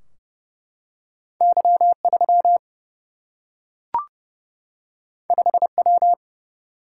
Morse code, heard as a keyed tone, about 30 wpm, 700 Hz. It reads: Y 3 5 W
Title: Single-Letter Single-Number 30wpm